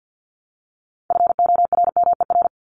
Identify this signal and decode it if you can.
Morse code, heard as a keyed tone, about 50 wpm, 700 Hz. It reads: FORMER
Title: LY Suffix - Top 100 Words with Base Words 50wpm